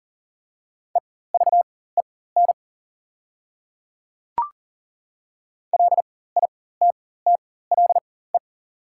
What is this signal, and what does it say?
Morse code, heard as a keyed tone, about 37 wpm, spaced as for 18 wpm, 690 Hz. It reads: EVEN LITTLE
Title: 4x Character Spacing - Top 200 Words - 40wpm